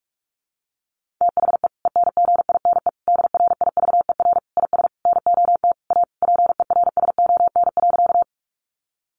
Morse code, heard as a keyed tone, about 44 wpm, 700 Hz. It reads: THE ENGINE DRIVER IS NOT A PERSON.
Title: Sentences from Top 1500 Words 45wpm